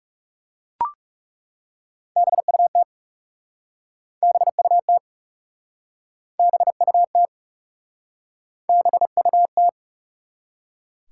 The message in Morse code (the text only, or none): BUT BUT BUT BUT